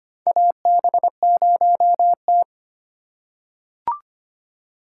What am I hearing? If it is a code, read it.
Morse code, heard as a keyed tone, about 25 wpm, 700 Hz. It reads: AB0T
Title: USA Call Signs - 4 Characters - 3x Character Spacing - 25wpm